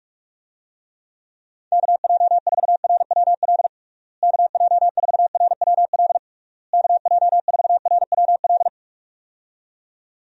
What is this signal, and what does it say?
Morse code, heard as a keyed tone, about 46 wpm, 700 Hz. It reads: KJ4RWL KJ4RWL KJ4RWL
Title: USA Call Signs - 6 Characters - Repeated 3x - 45wpm